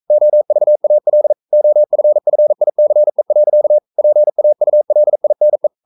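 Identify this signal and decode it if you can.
Morse code, heard as a keyed tone, about 42 wpm, 600 Hz. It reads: OVAL OFFICE. PAULINE
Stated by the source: Morse code practice at 41 wpm, from CW med Gnister.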